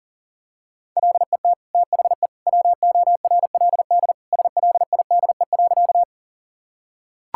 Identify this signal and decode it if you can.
Morse code, heard as a keyed tone, about 40 wpm, 700 Hz. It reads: LET THE WORLD SLIDE.
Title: Sentences from Top 1600 Words 40wpm